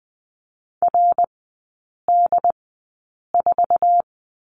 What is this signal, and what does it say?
Morse code, heard as a keyed tone, about 20 wpm, spaced as for 8 wpm, 700 Hz. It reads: RD4
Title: 067 Lesson 12e - Letter d - Three Characters 20wpm